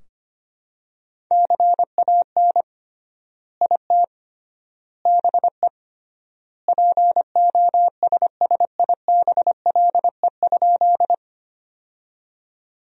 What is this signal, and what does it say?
Morse code, heard as a keyed tone, about 25 wpm, 700 Hz. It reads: CAN IT BE POSSIBLE?
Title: 3x Word-Spacing - Sentences from Top 500 Words - 25wpm